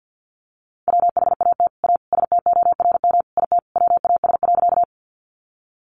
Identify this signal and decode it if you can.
Morse code, heard as a keyed tone, about 50 wpm, 700 Hz. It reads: WHAT A STORM IT WAS.
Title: Sentences from Top 1500 Words 50wpm